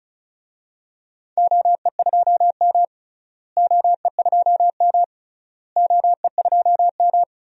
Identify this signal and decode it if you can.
Morse code, heard as a keyed tone, about 34 wpm, 700 Hz. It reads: OE2M OE2M OE2M
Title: DXCC-100 Call Signs - 4 Characters - Repeated 3x - 35wpm